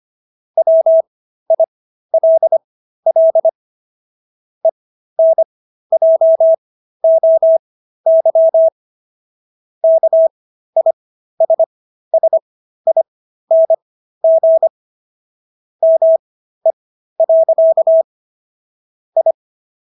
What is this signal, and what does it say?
Morse code, heard as a keyed tone, about 25 wpm, spaced as for 13 wpm, 650 Hz. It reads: WILL ENJOY KISSING ME. I